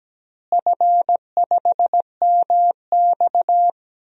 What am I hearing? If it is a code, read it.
Morse code, heard as a keyed tone, about 17 wpm, 700 Hz. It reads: F5MX
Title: DXCC-100 Call Signs - 4 Characters - Repeated 3x - 17wpm